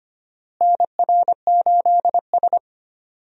Morse code, heard as a keyed tone, about 25 wpm, 700 Hz. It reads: NR8S
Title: USA Call Signs - 4 Characters - Repeated 2x - 25wpm